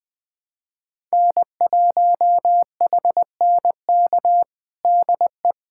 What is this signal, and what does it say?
Morse code, heard as a keyed tone, about 21 wpm, 700 Hz. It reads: N1HNK DE
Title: USA Callsign DE Callsign - 5 Characters - 20wpm